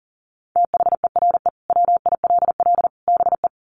Morse code, heard as a keyed tone, about 40 wpm, 700 Hz. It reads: THERE WILL BE
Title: Sets of 3 Words 40wpm